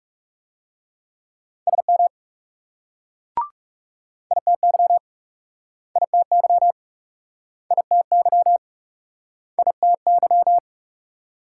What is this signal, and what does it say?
Morse code, heard as a keyed tone, about 40 wpm, 700 Hz. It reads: SM ITY ITY ITY ITY